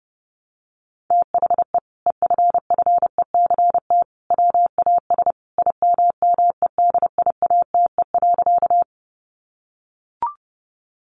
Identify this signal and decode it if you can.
Morse code, heard as a keyed tone, about 30 wpm, 700 Hz. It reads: THE EFFECT WAS IMMEDIATE.
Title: Sentences from Top 1200 Words 30wpm